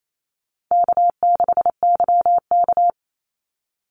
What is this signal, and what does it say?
Morse code, heard as a keyed tone, about 28 wpm, 700 Hz. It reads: K6YK
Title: NA SOTA Chaser Callsigns 28wpm